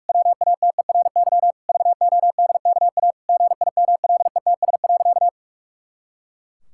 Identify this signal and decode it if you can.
Morse code, heard as a keyed tone, about 41 wpm, 700 Hz. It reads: WATERY VODKA GIMLETS.
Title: Pangram Challenge v2 - 45wpm